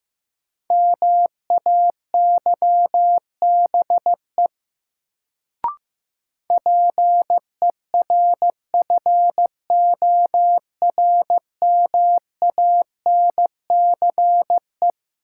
Morse code, heard as a keyed tone, about 15 wpm, 700 Hz. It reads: MAYBE PERFORMANCE
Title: Sentences from Top 1000 Words - New Word Drill - 15wpm